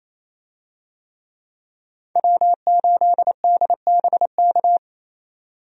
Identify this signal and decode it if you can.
Morse code, heard as a keyed tone, about 28 wpm, 700 Hz. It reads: W8DBK
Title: USA Call Signs - 5 Characters - 28wpm